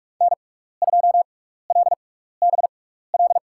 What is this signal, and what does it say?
Morse code, heard as a keyed tone, about 41 wpm, spaced as for 15 wpm, 700 Hz. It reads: N2LBL
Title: USA Call Signs - 6x Character Spacing - 45wpm